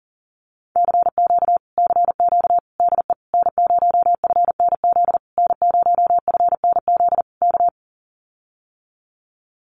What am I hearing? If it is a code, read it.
Morse code, heard as a keyed tone, about 39 wpm, 700 Hz. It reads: CQ CQ DE N0FNZ N0FNZ K
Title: Common QSO Phrases Encore 40wpm